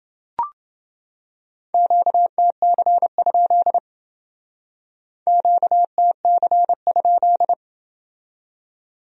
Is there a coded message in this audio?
Yes: Morse code from 1.7 to 7.5 s, heard as a keyed tone, about 29 wpm, 700 Hz. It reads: QTC? QTC?